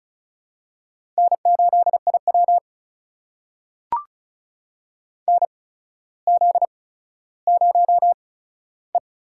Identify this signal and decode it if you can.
Morse code, heard as a keyed tone, about 34 wpm, spaced as for 15 wpm, 700 Hz. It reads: N8IW NZ0E
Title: USA Call Signs - 4 Characters - 8x Character Spacing - 35wpm